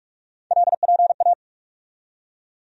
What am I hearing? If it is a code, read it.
Morse code, heard as a keyed tone, about 42 wpm, 700 Hz. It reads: LPA
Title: USA Call Sign Postfixes - 3 Letters 45wpm